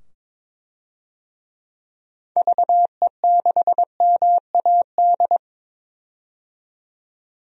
Morse code, heard as a keyed tone, about 22 wpm, 700 Hz. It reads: VE6MAD
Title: DXCC-100 Call Signs - 6 Characters - 22wpm